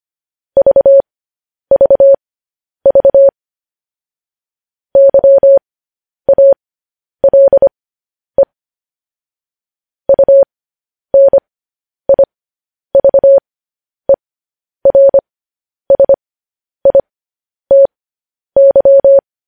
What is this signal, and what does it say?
Morse code, heard as a keyed tone, about 25 wpm, spaced as for 10 wpm, 550 Hz. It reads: VVV YALE UNIVERSITY
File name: podcast / News headlines-1x-25x10